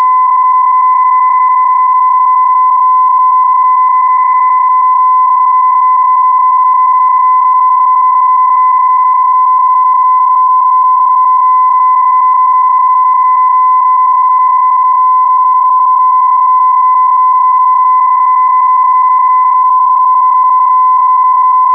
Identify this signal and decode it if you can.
PSK31, about 31.25 baud.